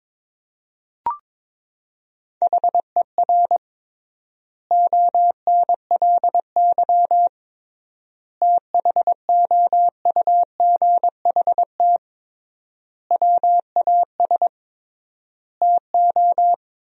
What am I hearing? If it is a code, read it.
Morse code, heard as a keyed tone, about 22 wpm, 700 Hz. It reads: HER ONLY THOUGHT WAS TO
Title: 3x Word-Spacing - Sentences from Top 400 Words - 22wpm